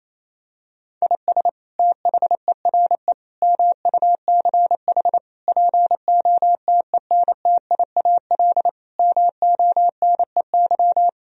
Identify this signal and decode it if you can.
Morse code, heard as a keyed tone, about 28 wpm, 700 Hz. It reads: IS THERE MUCH POTENTIAL MONEY
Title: Sentences from Top 1000 Words - Review Part 4 - 28wpm